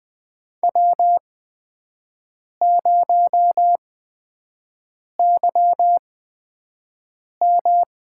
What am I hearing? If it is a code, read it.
Morse code, heard as a keyed tone, about 20 wpm, spaced as for 5 wpm, 700 Hz. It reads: W0YM
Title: USA Call Signs - 8x Character Spacing - 20wpm